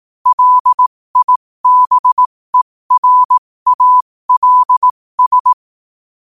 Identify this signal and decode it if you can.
Morse code, heard as a keyed tone, about 18 wpm, spaced as for 15 wpm, 1000 Hz. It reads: LIBERALS
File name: qotdfiles / 15wpm